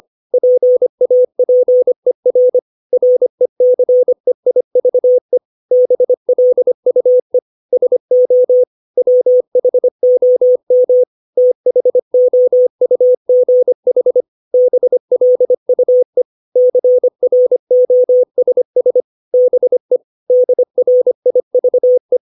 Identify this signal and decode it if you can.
Morse code, heard as a keyed tone, about 25 wpm, 500 Hz. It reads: PAPER RECEIVE BLUE SO WHOM THOUGH BLUE CROSS BE DRIVE
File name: 632 words / 632 Words 25WPM